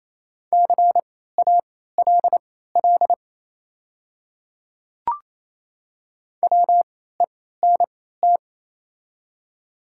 Morse code, heard as a keyed tone, about 28 wpm, spaced as for 16 wpm, 700 Hz. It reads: CALL WENT
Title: 3x Character Spacing - Top 200 Words - 28wpm